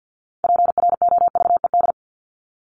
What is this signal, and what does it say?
Morse code, heard as a keyed tone, about 50 wpm, 700 Hz. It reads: PROVED
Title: ED Suffix - Top 100 Words with Base Words 50wpm